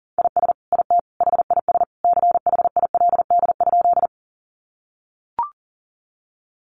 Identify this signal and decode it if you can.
Morse code, heard as a keyed tone, about 40 wpm, 700 Hz. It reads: IS IT HIS CHILD?
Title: Sentences from Top 800 Words 40wpm